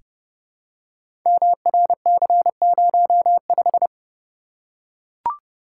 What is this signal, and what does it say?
Morse code, heard as a keyed tone, about 28 wpm, 700 Hz. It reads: MRC05
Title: Code Groups - 5 Letter:Numbers - 30wpm